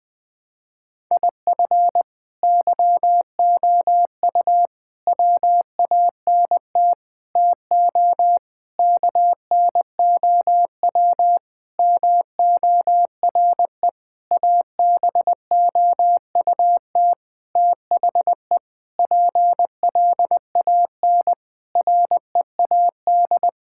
Morse code, heard as a keyed tone, about 20 wpm, 700 Hz. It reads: IF YOU WANT TO KNOW MORE ABOUT THE PLAN READ
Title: Sentences with 8-Word Phrases - 20wpm